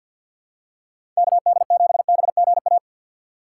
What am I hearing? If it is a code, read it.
Morse code, heard as a keyed tone, about 51 wpm, 700 Hz. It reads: KD7BGA
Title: USA Call Signs - 6 Characters - 3x Character Spacing - 50wpm